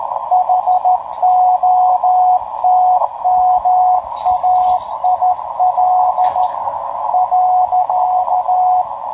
Morse code, heard as a keyed tone, about 13 wpm, 730 Hz, with radio noise. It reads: HOTMAIL .
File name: tracks / hp1rcp